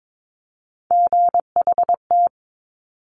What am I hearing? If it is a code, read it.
Morse code, heard as a keyed tone, about 22 wpm, 700 Hz. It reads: GHT
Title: Top Three Letter Combinations in Words 22wpm